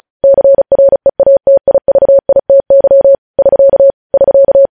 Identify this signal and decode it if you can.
Morse code, heard as a keyed tone, about 36 wpm, 550 Hz. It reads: CREATIVITY <SK> <SK>
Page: News Headlines in Morse Code at 35 WPM